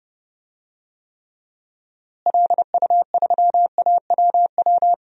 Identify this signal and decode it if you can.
Morse code, heard as a keyed tone, about 30 wpm, 700 Hz. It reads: LU3AWW
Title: DXCC-100 Call Signs - 6 Characters - Repeated 2x - 30wpm